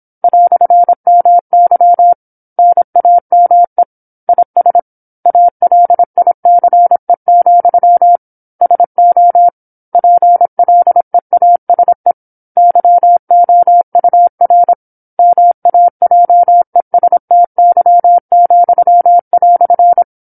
Morse code, heard as a keyed tone, about 27 wpm, 700 Hz. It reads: "MY NAME IS ALICE, SO PLEASE YOUR MAJESTY,"
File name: 08-Chapter-08 26 WPM